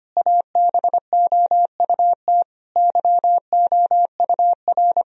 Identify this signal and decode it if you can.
Morse code, heard as a keyed tone, about 25 wpm, 700 Hz. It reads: ABOUT YOUR
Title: Sentences from Top 1000 Words - Review Part 3 - 25wpm